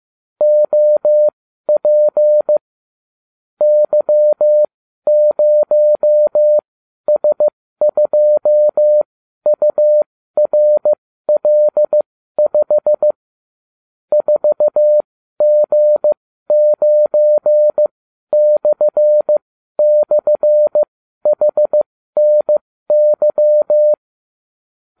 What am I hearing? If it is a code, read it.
Morse code, heard as a keyed tone, about 15 wpm, spaced as for 12 wpm, 600 Hz. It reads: OP Y0S2URL5 4G9//HNY
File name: cw / ubacw34 3